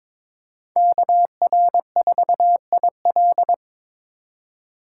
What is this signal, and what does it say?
Morse code, heard as a keyed tone, about 22 wpm, 700 Hz. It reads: KR4IL
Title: USA Call Signs - 5 Characters - 3x Character Spacing - 22wpm